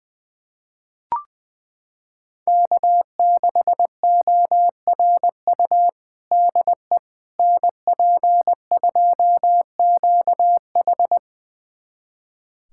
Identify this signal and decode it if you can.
Morse code, heard as a keyed tone, about 20 wpm, 700 Hz. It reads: K6ORU DE NP2QH
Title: USA Callsign DE Callsign - 5 Characters - 20wpm